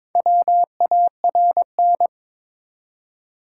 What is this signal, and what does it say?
Morse code, heard as a keyed tone, about 22 wpm, 700 Hz. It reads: WARN
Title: Four Letter Words - 22wpm